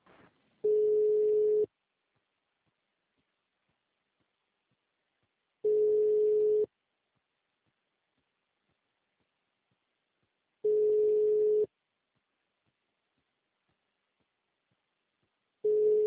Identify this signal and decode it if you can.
Ringback tone (Europe, 425 Hz).
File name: out- -2057- - - .1622